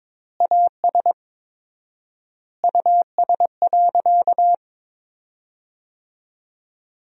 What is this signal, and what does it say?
Morse code, heard as a keyed tone, about 22 wpm, 700 Hz. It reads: AS US.
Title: 4x Word-Spacing - Sentences from Top 500 Words - 22wpm